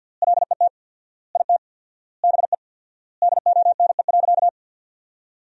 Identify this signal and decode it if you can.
Morse code, heard as a keyed tone, about 52 wpm, 700 Hz. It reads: LET IT BE DONE.
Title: 4x Word-Spacing - Sentences from Top 500 Words - 50wpm